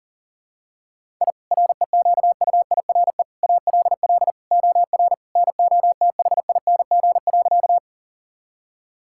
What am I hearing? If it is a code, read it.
Morse code, heard as a keyed tone, about 40 wpm, 700 Hz. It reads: I REQUIRE ALL OR NOTHING.
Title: Sentences from Top 1000 Words - Review Part 6 - 40wpm